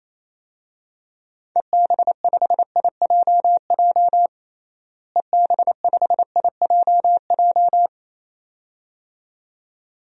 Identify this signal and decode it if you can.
Morse code, heard as a keyed tone, about 28 wpm, 700 Hz. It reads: EB5IJJ EB5IJJ
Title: DXCC-100 Call Signs - 6 Characters - Repeated 2x - 28wpm